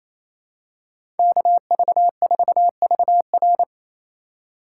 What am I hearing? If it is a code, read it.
Morse code, heard as a keyed tone, about 28 wpm, 700 Hz. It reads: KV4VR
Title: USA Call Signs - 5 Characters - 2x Character Spacing - 28wpm